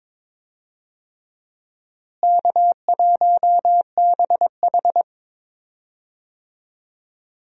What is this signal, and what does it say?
Morse code, heard as a keyed tone, about 22 wpm, 700 Hz. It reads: K1BH
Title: USA Call Signs - 4 Characters - Repeated 3x - 22wpm